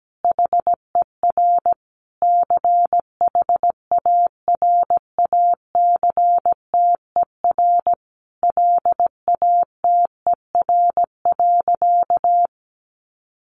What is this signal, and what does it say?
Morse code, heard as a keyed tone, about 17 wpm, 700 Hz. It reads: HER CHARACTER LATER.
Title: Sentences from Top 1200 Words 17wpm